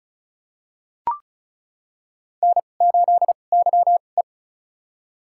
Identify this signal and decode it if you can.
Morse code, heard as a keyed tone, about 34 wpm, spaced as for 25 wpm, 700 Hz. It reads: N8YE
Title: USA Call Signs - 2x Character Spacing - 35wpm